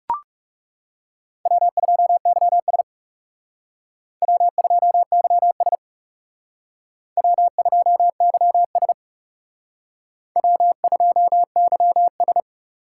Morse code, heard as a keyed tone, about 36 wpm, 700 Hz. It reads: W2YS W2YS W2YS W2YS
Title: Speed Racing - USA Call Signs 30wpm